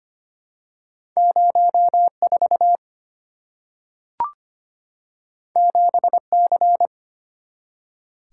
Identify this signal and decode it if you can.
Morse code, heard as a keyed tone, about 25 wpm, 700 Hz. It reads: 04 7C
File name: Two Letter-Number Combinations - 25wpm